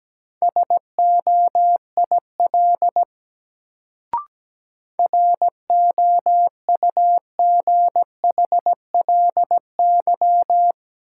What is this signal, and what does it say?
Morse code, heard as a keyed tone, about 17 wpm, 700 Hz. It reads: SOIL ROUGHLY
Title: Sentences from Top 1500 Words - New Word Drill Alt - 17wpm